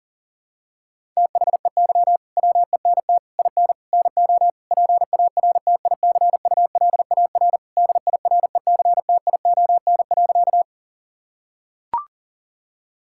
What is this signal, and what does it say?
Morse code, heard as a keyed tone, about 40 wpm, 700 Hz. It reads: THEY WENT IN NO PARTICULAR DIRECTION.
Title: Sentences from Top 1000 Words - Review Part 5 - 40wpm